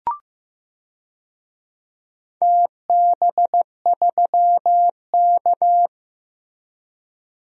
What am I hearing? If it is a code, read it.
Morse code, heard as a keyed tone, about 15 wpm, 700 Hz. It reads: TB3K
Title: DXCC-100 Call Signs - 4 Characters - Repeated 3x - 15wpm